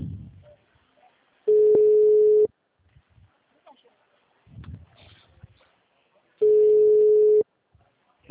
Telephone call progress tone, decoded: ringback tone

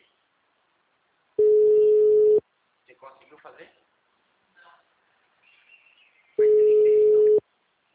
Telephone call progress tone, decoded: ringback tone

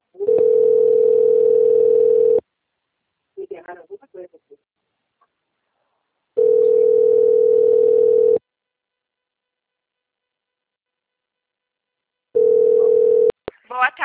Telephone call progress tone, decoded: ringback tone

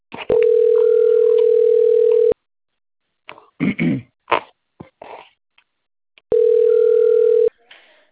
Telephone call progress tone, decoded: ringback tone

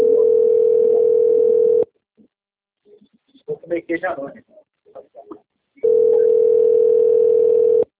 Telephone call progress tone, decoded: ringback tone

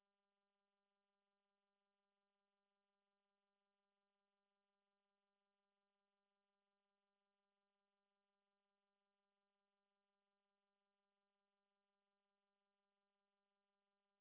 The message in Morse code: none

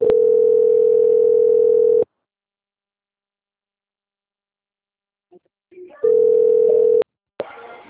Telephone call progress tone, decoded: ringback tone

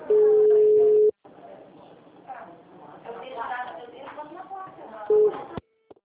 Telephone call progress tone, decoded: ringback tone

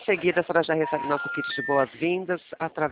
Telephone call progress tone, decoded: special information tone

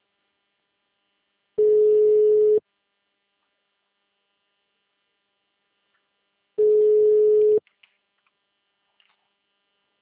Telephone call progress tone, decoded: ringback tone

